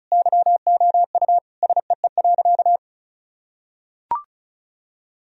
Morse code, heard as a keyed tone, about 36 wpm, 700 Hz. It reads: YOU SEE.
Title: Sentences from Top 1000 Words - Review Part 2 - 35wpm